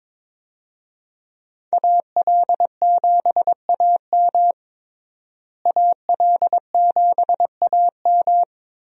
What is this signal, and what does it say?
Morse code, heard as a keyed tone, about 22 wpm, 700 Hz. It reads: AL7AM AL7AM
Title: USA Call Signs - 5 Characters - Repeated 2x - 22wpm